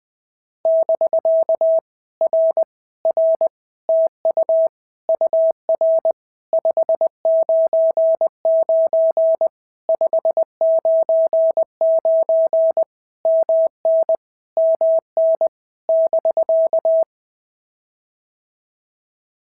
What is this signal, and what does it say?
Morse code, heard as a keyed tone, about 21 wpm, 650 Hz. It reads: <BK> R R TU UR 599 599 MN MN <BK>
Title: POTA QSO Practice - 20wpm